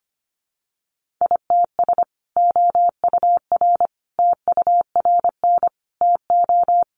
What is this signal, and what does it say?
Morse code, heard as a keyed tone, about 25 wpm, 700 Hz. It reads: ITS OUR TURN TO
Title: Sentences from Top 600 Words 25wpm